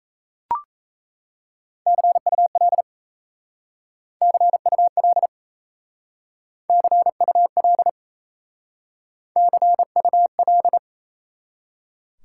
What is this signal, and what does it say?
Morse code, heard as a keyed tone, about 35 wpm, 700 Hz. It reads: CUL CUL CUL CUL